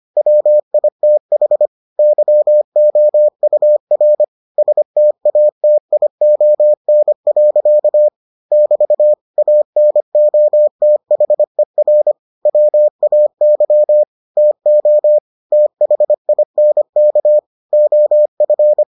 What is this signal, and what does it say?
Morse code, heard as a keyed tone, about 25 wpm, 600 Hz. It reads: WITH YOUR STATION. <BT> ANOTHER WAY TO THINK OF